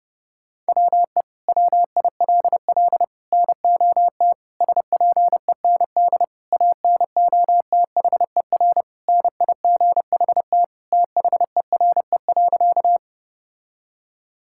Morse code, heard as a keyed tone, about 30 wpm, 700 Hz. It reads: WE WILL NOT SPEND ANOTHER NIGHT THERE.
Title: Sentences from Top 1000 Words - Review Part 5 - 30wpm